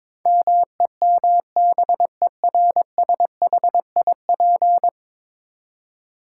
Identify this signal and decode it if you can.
Morse code, heard as a keyed tone, about 22 wpm, 700 Hz. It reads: MEMBERSHIP
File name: practice / 22wpm / 10 Letter Words - First 3 Letters Given - 22wpm